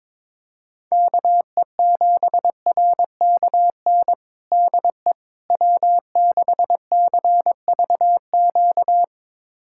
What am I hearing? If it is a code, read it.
Morse code, heard as a keyed tone, about 22 wpm, 700 Hz. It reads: KE7RKN DE W6CVQ
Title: USA Callsign DE Callsign 22wpm